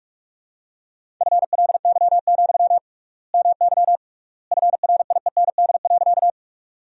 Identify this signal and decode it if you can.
Morse code, heard as a keyed tone, about 47 wpm, 700 Hz. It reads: FLY, MY FRIEND.
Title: 3x Word-Spacing - Sentences from Top 500 Words - 45wpm